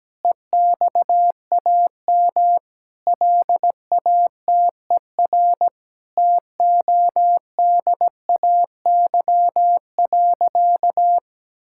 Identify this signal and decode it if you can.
Morse code, heard as a keyed tone, about 17 wpm, 700 Hz. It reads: EXAM LATER TODAY.
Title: Sentences from Top 1000 Words - Review Part 5 - 17wpm